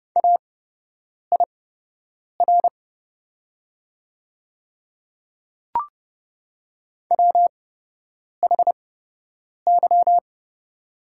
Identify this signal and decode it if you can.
Morse code, heard as a keyed tone, about 29 wpm, spaced as for 8 wpm, 700 Hz. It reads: AIR WHY
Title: 8x Character Spacing - Top 200 Words - 30wpm